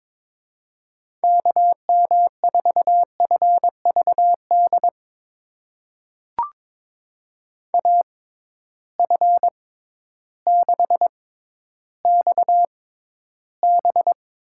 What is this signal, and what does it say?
Morse code, heard as a keyed tone, about 22 wpm, spaced as for 12 wpm, 700 Hz. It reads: KM4FVD AF6XB
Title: USA Call Signs - 6x Character Spacing - 22wpm